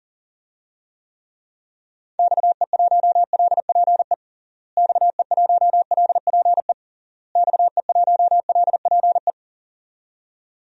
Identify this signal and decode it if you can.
Morse code, heard as a keyed tone, about 40 wpm, 700 Hz. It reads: XE1LPE XE1LPE XE1LPE
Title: DXCC-100 Call Signs - 6 Characters - Repeated 3x - 40wpm